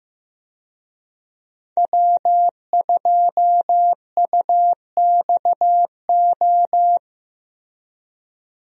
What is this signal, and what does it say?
Morse code, heard as a keyed tone, about 15 wpm, 700 Hz. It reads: W2UXO
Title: USA Call Signs - 5 Characters - Repeated 2x - 15wpm